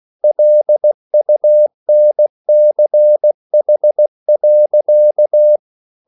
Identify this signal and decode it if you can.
Morse code, heard as a keyed tone, about 16 wpm, 600 Hz. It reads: LUNCH.
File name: CW / Practice / EnglishSayings16